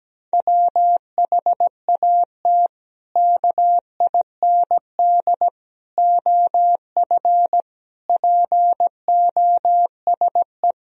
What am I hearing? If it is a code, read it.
Morse code, heard as a keyed tone, about 17 wpm, 700 Hz. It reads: WHAT KIND OF POSE
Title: Sentences from Top 1000 Words - Review Part 2 - 17wpm